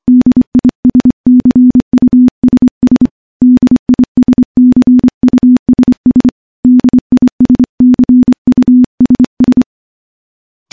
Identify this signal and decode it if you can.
Morse code, heard as a keyed tone, about 25 wpm, 250 Hz. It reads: DISCUSS DISCUSS DISCUSS
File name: morse / morse-training-25WPM